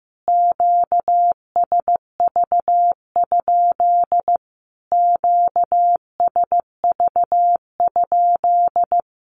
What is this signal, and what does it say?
Morse code, heard as a keyed tone, about 15 wpm, 700 Hz. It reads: QSV? QSV?
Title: Q-Codes Repeated 3x 15wpm